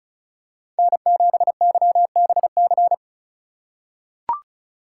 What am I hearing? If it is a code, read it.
Morse code, heard as a keyed tone, about 34 wpm, 700 Hz. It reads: N7YBC
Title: USA Call Signs - 5 Characters - 2x Character Spacing - 35wpm